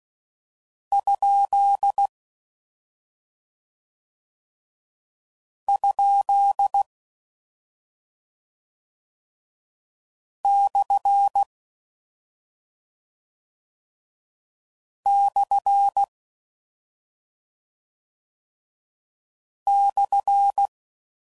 Morse code, heard as a keyed tone, about 16 wpm, 800 Hz. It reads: ? ? / / /
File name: Mixed / Misc / Punct